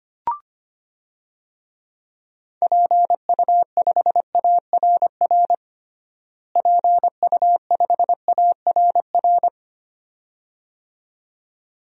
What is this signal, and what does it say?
Morse code, heard as a keyed tone, about 25 wpm, 700 Hz. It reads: PU5ARR PU5ARR
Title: DXCC-100 Call Signs - 6 Characters - Repeated 2x - 25wpm